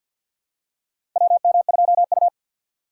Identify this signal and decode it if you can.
Morse code, heard as a keyed tone, about 50 wpm, 700 Hz. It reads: WM2U
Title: USA Call Signs - 4 Characters - 6x Character Spacing - 50wpm